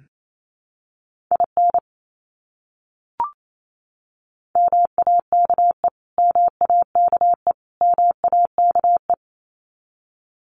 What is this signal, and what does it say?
Morse code, heard as a keyed tone, about 28 wpm, 700 Hz. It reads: IN MAKE MAKE MAKE
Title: Top 300 Words Repeated 28wpm